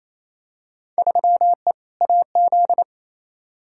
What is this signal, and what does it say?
Morse code, heard as a keyed tone, about 28 wpm, 700 Hz. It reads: 3E AZ
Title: Field Day Exchanges Encore - 28wpm